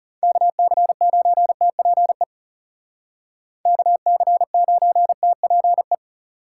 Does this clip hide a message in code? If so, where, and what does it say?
Yes: Morse code from 0.2 to 5.9 s, heard as a keyed tone, about 39 wpm, 700 Hz. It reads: KC9TPE KC9TPE